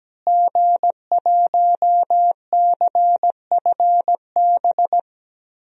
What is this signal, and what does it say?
Morse code, heard as a keyed tone, about 17 wpm, 700 Hz. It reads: G1CFB
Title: DXCC-100 Call Signs - 5 Characters - Repeated 2x - 17wpm